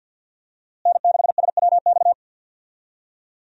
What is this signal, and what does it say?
Morse code, heard as a keyed tone, about 47 wpm, 700 Hz. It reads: N6SWX
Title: USA Call Signs - 5 Characters - Repeated 2x - 50wpm